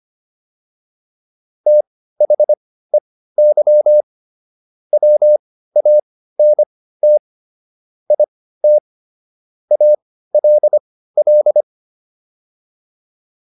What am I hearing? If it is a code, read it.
Morse code, heard as a keyed tone, about 25 wpm, spaced as for 15 wpm, 600 Hz. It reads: THEY WANT IT ALL